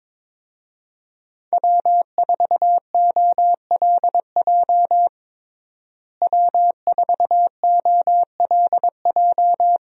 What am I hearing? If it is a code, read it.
Morse code, heard as a keyed tone, about 23 wpm, 700 Hz. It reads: W4OLJ W4OLJ
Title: USA Call Signs - 5 Characters - Repeated 2x - 22wpm